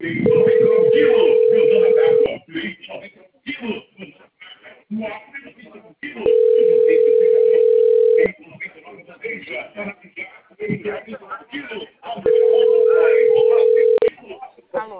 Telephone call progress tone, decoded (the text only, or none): ringback tone